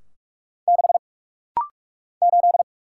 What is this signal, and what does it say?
Morse code, heard as a keyed tone, about 44 wpm, 700 Hz. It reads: 6 8
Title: Single Letter-Number - Rapid-Fire 45wpm